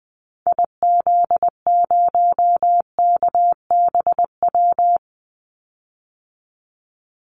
Morse code, heard as a keyed tone, about 19 wpm, 700 Hz. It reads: IZ0KBW
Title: Call Signs Repeated 2x - 20wpm